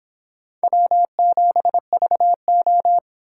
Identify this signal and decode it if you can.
Morse code, heard as a keyed tone, about 29 wpm, 700 Hz. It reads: W7VO